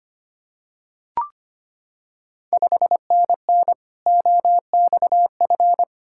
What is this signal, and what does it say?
Morse code, heard as a keyed tone, about 25 wpm, 690 Hz. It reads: 5NN OXF
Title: Ontario QSO Party Exchange 25wpm